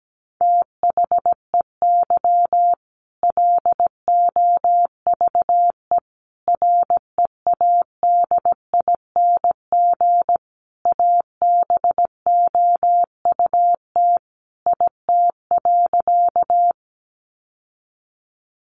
Morse code, heard as a keyed tone, about 17 wpm, 700 Hz. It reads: THEY LOVE READING ABOUT IT.
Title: Sentences from Top 600 Words 17wpm